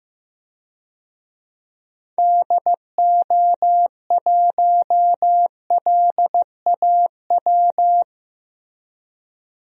Morse code, heard as a keyed tone, about 15 wpm, 700 Hz. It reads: DO1LAW
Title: DXCC-100 Call Signs - 6 Characters - Repeated 2x - 15wpm